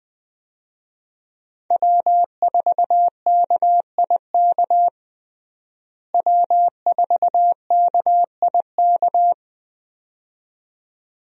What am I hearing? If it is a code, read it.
Morse code, heard as a keyed tone, about 20 wpm, 700 Hz. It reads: W4KIK W4KIK
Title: USA Call Signs - 5 Characters - Repeated 2x - 20wpm